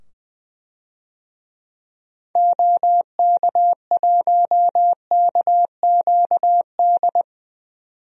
Morse code, heard as a keyed tone, about 20 wpm, 700 Hz. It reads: OK1KQD
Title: DXCC-100 Call Signs - 6 Characters - 20wpm